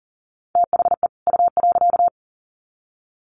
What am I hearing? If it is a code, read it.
Morse code, heard as a keyed tone, about 40 wpm, 700 Hz. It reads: THE U.
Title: Sets of 2 Words 40wpm